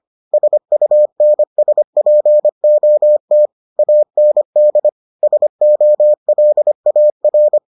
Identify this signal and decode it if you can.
Morse code, heard as a keyed tone, about 25 wpm, 600 Hz. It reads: SUNSPOT AND SOLAR